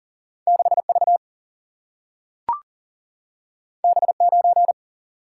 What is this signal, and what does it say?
Morse code, heard as a keyed tone, about 40 wpm, 700 Hz. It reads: 6V B9
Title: Single-Letter Single-Number 40wpm